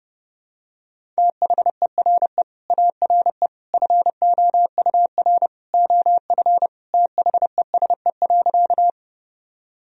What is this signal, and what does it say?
Morse code, heard as a keyed tone, about 30 wpm, 700 Hz. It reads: THERE ARE FOUR OF THESE.
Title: Sentences from Top 1000 Words - Review Part 1 - 30wpm